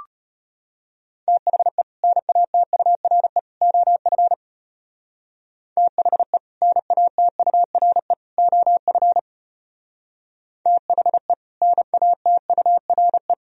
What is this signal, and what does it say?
Morse code, heard as a keyed tone, about 34 wpm, 700 Hz. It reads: THE NATURE OF THE NATURE OF THE NATURE